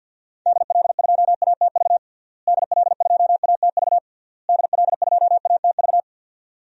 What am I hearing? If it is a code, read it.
Morse code, heard as a keyed tone, about 52 wpm, 700 Hz. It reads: DL2ATV DL2ATV DL2ATV
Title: DXCC-100 Call Signs - 6 Characters - Repeated 3x - 50wpm